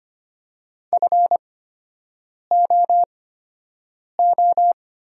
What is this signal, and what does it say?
Morse code, heard as a keyed tone, about 25 wpm, spaced as for 7 wpm, 700 Hz. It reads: FOO
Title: 8x Character Spacing - Top 500 Words - 25wpm